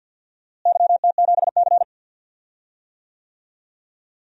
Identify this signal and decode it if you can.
Morse code, heard as a keyed tone, about 48 wpm, 700 Hz. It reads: YT7C
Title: DXCC-100 Call Signs - 4 Characters - Repeated 3x - 50wpm